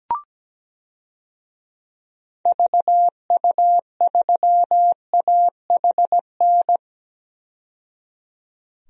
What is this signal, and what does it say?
Morse code, heard as a keyed tone, about 17 wpm, 700 Hz. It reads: VU3AHN
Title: DXCC-100 Call Signs - 6 Characters - 17wpm